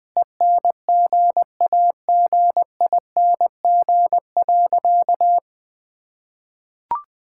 Morse code, heard as a keyed tone, about 20 wpm, 700 Hz. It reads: ENGAGING.
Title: Sentences with 2-Word Phrases - Spoken Beginning - 20wpm